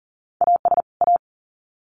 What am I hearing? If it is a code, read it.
Morse code, heard as a keyed tone, about 40 wpm, 700 Hz. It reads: AS A
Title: Sets of 2 Words 40wpm